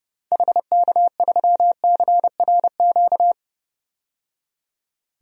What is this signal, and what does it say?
Morse code, heard as a keyed tone, about 30 wpm, 700 Hz. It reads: HK3CRQ
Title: DXCC-100 Call Signs - 6 Characters - 30wpm